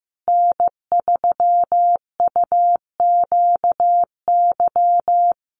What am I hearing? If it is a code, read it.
Morse code, heard as a keyed tone, about 15 wpm, 700 Hz. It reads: N3UQY
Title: USA Call Signs Repeated 3x 15wpm